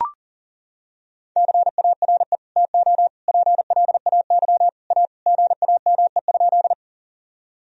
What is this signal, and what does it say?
Morse code, heard as a keyed tone, about 37 wpm, 700 Hz. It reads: CARE TO PLAY A GAME?
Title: Sentences from Top 1000 Words - Review Part 1 - 40wpm